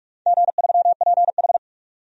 Morse code, heard as a keyed tone, about 46 wpm, 700 Hz. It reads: G3PH
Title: DXCC-100 Call Signs - 4 Characters - 45wpm